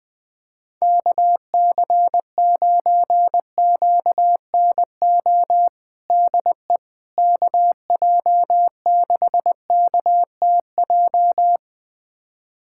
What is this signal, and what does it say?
Morse code, heard as a keyed tone, about 20 wpm, 700 Hz. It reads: KC9QNO DE KJ6KTJ
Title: USA Callsign DE Callsign 20wpm